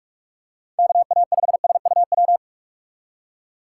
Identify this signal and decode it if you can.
Morse code, heard as a keyed tone, about 47 wpm, 700 Hz. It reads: KA5SUW
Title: USA Call Signs - 2x Character Spacing - 45wpm